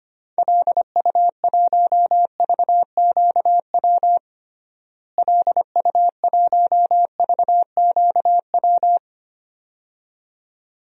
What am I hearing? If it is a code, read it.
Morse code, heard as a keyed tone, about 25 wpm, 700 Hz. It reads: LU1VQW LU1VQW
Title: DXCC-100 Call Signs - 6 Characters - Repeated 2x - 25wpm